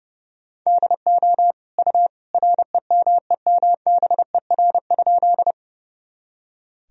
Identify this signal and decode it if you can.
Morse code, heard as a keyed tone, about 32 wpm, 700 Hz. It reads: DO U REMEMBER?
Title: Sentences from Top 400 Words and Abbreviations 30wpm